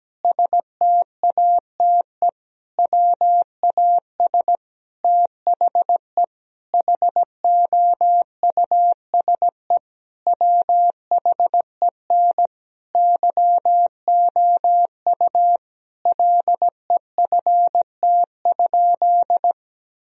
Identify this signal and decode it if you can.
Morse code, heard as a keyed tone, about 17 wpm, 700 Hz. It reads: STATE WAS THE HOUSE WHEN YOU LEFT?
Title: Sentences from Top 1000 Words - Review Part 2 - 17wpm